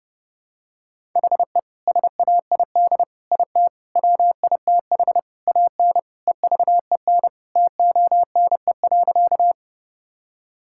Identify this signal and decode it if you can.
Morse code, heard as a keyed tone, about 30 wpm, 700 Hz. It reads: HE SAID IT WITH AN EVEN TONE.
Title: Sentences from Top 1600 Words 30wpm